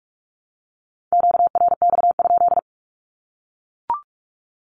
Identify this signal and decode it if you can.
Morse code, heard as a keyed tone, about 44 wpm, 700 Hz. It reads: QRX?
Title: Q-Codes 45wpm